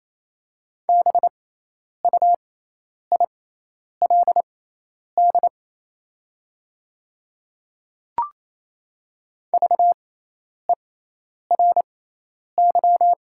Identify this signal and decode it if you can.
Morse code, heard as a keyed tone, about 28 wpm, spaced as for 10 wpm, 700 Hz. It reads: BUILD VERY
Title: 6x Character Spacing - Top 200 Words - 28wpm